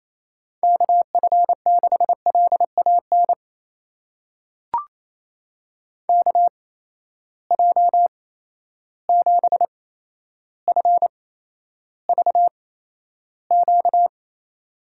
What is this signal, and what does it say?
Morse code, heard as a keyed tone, about 28 wpm, spaced as for 12 wpm, 700 Hz. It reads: KF6LAN KJ7FVQ
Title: USA Call Signs - 6 Characters - 8x Character Spacing - 28wpm